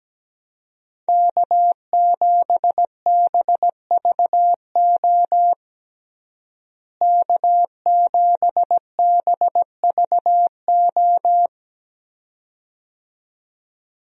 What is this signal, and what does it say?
Morse code, heard as a keyed tone, about 17 wpm, 700 Hz. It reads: K7BVO K7BVO
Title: USA Call Signs - 5 Characters - Repeated 2x - 17wpm